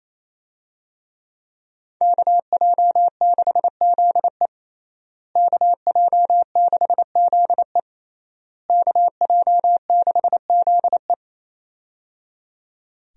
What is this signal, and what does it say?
Morse code, heard as a keyed tone, about 27 wpm, 700 Hz. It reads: KJ6ZE KJ6ZE KJ6ZE
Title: USA Call Signs - 5 Characters - Repeated 3x - 28wpm